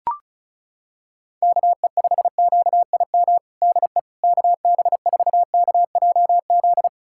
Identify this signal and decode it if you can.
Morse code, heard as a keyed tone, about 36 wpm, 700 Hz. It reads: KE5QIM DE KB4KJZ
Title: USA Callsign DE Callsign - 6 Characters - 35wpm